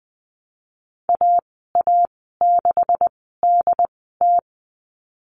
Morse code, heard as a keyed tone, about 20 wpm, spaced as for 15 wpm, 700 Hz. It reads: AA6DT
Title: 147 Lesson 24g - Letter y - Call Signs 20wpm